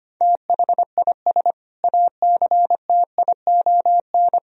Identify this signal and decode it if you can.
Morse code, heard as a keyed tone, about 25 wpm, 700 Hz. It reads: THIS ACTION
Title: Sentences from Top 1000 Words - Review Part 4 - 25wpm